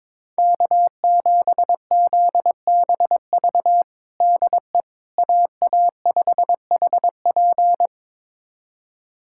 Morse code, heard as a keyed tone, about 22 wpm, 700 Hz. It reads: K7ZBV DE AA5HP
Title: USA Callsign DE Callsign - 5 Characters - 22wpm